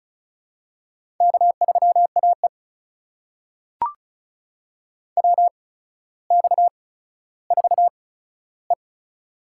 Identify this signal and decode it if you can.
Morse code, heard as a keyed tone, about 36 wpm, spaced as for 15 wpm, 700 Hz. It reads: K3AE WX4E
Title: USA Call Signs - 4 Characters - 8x Character Spacing - 35wpm